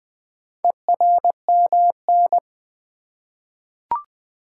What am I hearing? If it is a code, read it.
Morse code, heard as a keyed tone, about 20 wpm, 700 Hz. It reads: ERMN
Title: Four Letter-Number Combinations - 20wpm